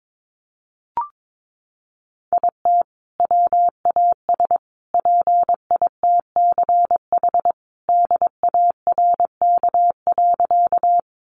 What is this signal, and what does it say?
Morse code, heard as a keyed tone, about 22 wpm, 700 Hz. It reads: IT WAS PITCH DARK.
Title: Sentences from Top 1300 Words 22wpm